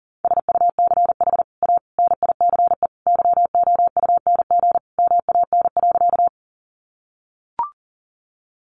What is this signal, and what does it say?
Morse code, heard as a keyed tone, about 40 wpm, 700 Hz. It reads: SUCH A NICE YOUNG MAN.
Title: Sentences from Top 1100 Words 40wpm